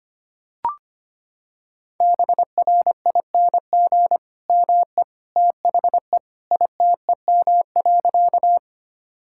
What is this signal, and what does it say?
Morse code, heard as a keyed tone, about 25 wpm, 700 Hz. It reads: BRING ME THE ITEM.
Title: Sentences from Top 1000 Words - Review Part 6 - 25wpm